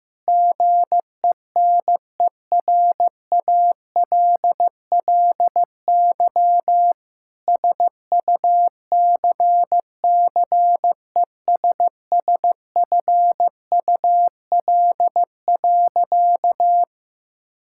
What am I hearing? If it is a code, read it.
Morse code, heard as a keyed tone, about 15 wpm, 700 Hz. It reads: GENERALLY SUCCESSFUL.
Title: Sentences from Top 1000 Words - Review Part 6 - 15wpm